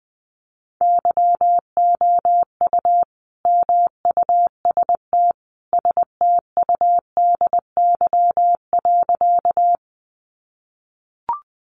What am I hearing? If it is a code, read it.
Morse code, heard as a keyed tone, about 20 wpm, 700 Hz. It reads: YOU MUST STUDY.
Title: Sentences from Top 300 Words 20wpm